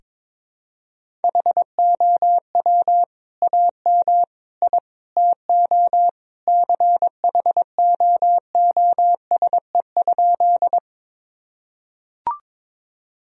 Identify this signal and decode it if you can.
Morse code, heard as a keyed tone, about 22 wpm, 700 Hz. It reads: HOW AM I TO CHOOSE?
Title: Sentences from Top 1000 Words - Review Part 4 - 22wpm